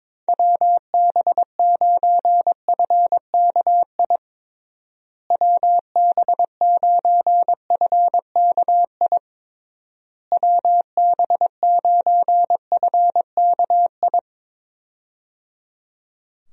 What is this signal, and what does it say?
Morse code, heard as a keyed tone, about 22 wpm, 700 Hz. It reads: WB9FKI WB9FKI WB9FKI
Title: USA Call Signs - 6 Characters - Repeated 3x - 22wpm